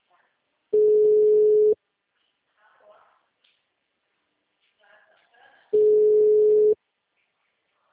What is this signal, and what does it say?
Ringback tone (Europe, 425 Hz), with speech.